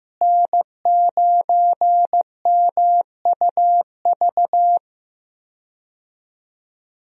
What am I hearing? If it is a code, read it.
Morse code, heard as a keyed tone, about 15 wpm, 700 Hz. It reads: N9MUV
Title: USA Call Signs - 5 Characters - Repeated 2x - 15wpm